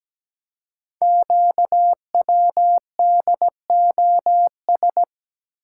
Morse code, heard as a keyed tone, about 17 wpm, 700 Hz. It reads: QWDOS
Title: Code Groups - 5 Letters - 17wpm